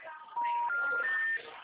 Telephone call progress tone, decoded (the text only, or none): special information tone